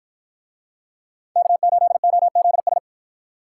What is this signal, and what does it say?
Morse code, heard as a keyed tone, about 53 wpm, 700 Hz. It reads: K8OZS